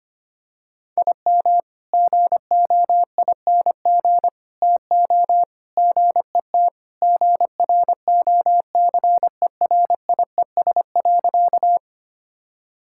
Morse code, heard as a keyed tone, about 25 wpm, 700 Hz. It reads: IM GOING TO GET GROCERIES.
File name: practice / 25wpm / Sentences with 3-Word Phrases - 25wpm